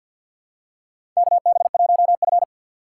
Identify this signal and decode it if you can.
Morse code, heard as a keyed tone, about 47 wpm, 700 Hz. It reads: KB1F
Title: USA Call Signs - 4 Characters - 6x Character Spacing - 50wpm